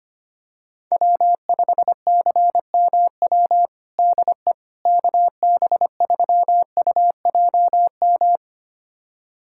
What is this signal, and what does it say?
Morse code, heard as a keyed tone, about 25 wpm, 700 Hz. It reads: W5CMW DE KB3UJM
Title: USA Callsign DE Callsign 25wpm